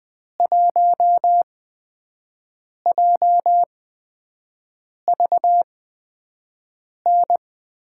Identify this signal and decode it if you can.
Morse code, heard as a keyed tone, about 21 wpm, spaced as for 5 wpm, 700 Hz. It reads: 1JVN
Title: USA Call Signs - 6 Characters - 8x Character Spacing - 20wpm